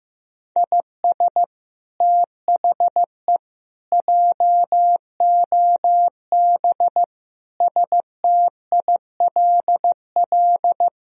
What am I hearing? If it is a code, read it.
Morse code, heard as a keyed tone, about 15 wpm, 700 Hz. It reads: IS THE JOB STILL
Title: Sentences from Top 1000 Words - Review Part 3 - 15wpm